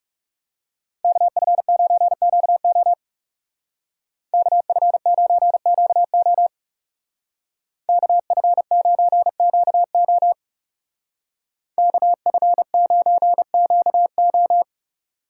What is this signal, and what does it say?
Morse code, heard as a keyed tone, about 37 wpm, 700 Hz. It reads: KF9QO KF9QO KF9QO KF9QO
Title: Speed Racing - USA 2x2 Call Signs 30wpm